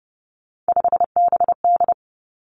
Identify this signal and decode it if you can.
Morse code, heard as a keyed tone, about 32 wpm, 700 Hz. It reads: 5BD
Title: Three Challenging Letter-Number Combinations 30wpm